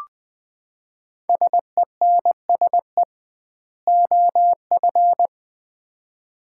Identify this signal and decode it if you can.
Morse code, heard as a keyed tone, about 20 wpm, 700 Hz. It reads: SENSE OF
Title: 2x Word-Spacing - Sets of 2 Words - 20wpm